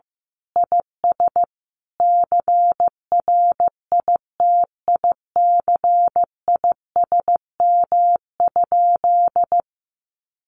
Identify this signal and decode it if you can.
Morse code, heard as a keyed tone, about 15 wpm, 700 Hz. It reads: IS CRITICISM?
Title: Sentences from Top 1300 Words 15wpm